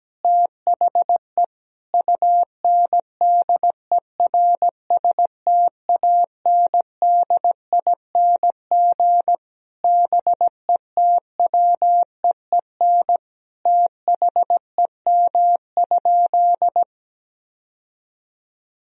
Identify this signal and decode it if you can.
Morse code, heard as a keyed tone, about 17 wpm, 700 Hz. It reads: THE UNDERSTANDING BETWEEN THEM?
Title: Sentences from Top 1000 Words - Review Part 3 - 17wpm